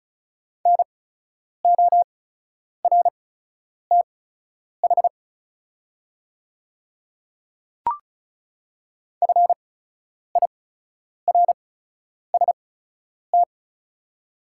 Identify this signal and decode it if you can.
Morse code, heard as a keyed tone, about 36 wpm, spaced as for 10 wpm, 700 Hz. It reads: NORTH FIRST
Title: 8x Character Spacing - Top 300 Words - 35wpm